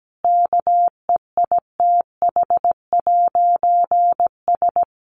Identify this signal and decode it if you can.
Morse code, heard as a keyed tone, about 17 wpm, 700 Hz. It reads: KEITH'S
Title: A Sound Of Thunder 17wpm